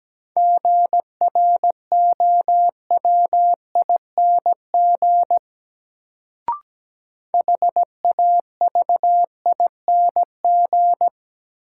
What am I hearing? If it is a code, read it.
Morse code, heard as a keyed tone, about 17 wpm, 700 Hz. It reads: GROWING HAVING
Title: Sets of 5 Words Encore - New Word Drill - 17wpm